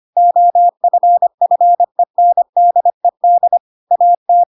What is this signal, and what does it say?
Morse code, heard as a keyed tone, about 25 wpm, 700 Hz. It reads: OFFENDED AT
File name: Pinocchio / Pinocchio ch4 25 25 ik1hge